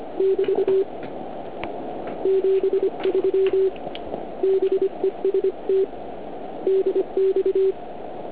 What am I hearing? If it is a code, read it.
Morse code, heard as a keyed tone, about 25 wpm, 380 Hz. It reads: <BT> 73 BEST DX